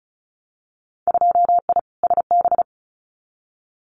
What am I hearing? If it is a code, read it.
Morse code, heard as a keyed tone, about 34 wpm, 700 Hz. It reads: 2I SB
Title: Winter Field Day Exchanges Encore 35wpm